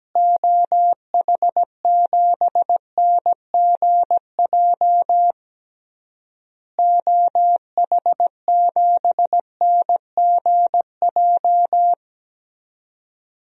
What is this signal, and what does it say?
Morse code, heard as a keyed tone, about 17 wpm, 700 Hz. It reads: OH7NGJ OH7NGJ
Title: DXCC-100 Call Signs - 6 Characters - Repeated 3x - 17wpm